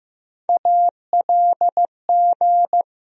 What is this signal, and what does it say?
Morse code, heard as a keyed tone, about 15 wpm, 700 Hz. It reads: ALG
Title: USA Call Sign Postfixes - 3 Letters 15wpm